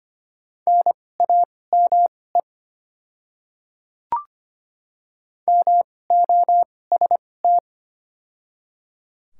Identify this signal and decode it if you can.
Morse code, heard as a keyed tone, about 25 wpm, spaced as for 18 wpm, 700 Hz. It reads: NAME MOST
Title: 2x Character Spacing - Top 200 Words - 25wpm